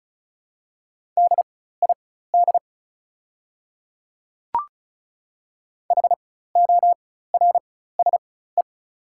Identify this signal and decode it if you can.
Morse code, heard as a keyed tone, about 36 wpm, spaced as for 16 wpm, 690 Hz. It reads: DID HORSE
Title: 4x Character Spacing - Top 300 Words - 35wpm